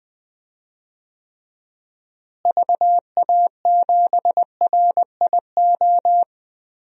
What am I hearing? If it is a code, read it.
Morse code, heard as a keyed tone, about 21 wpm, 700 Hz. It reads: VA7RIO
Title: DXCC-100 Call Signs - 6 Characters - Repeated 2x - 20wpm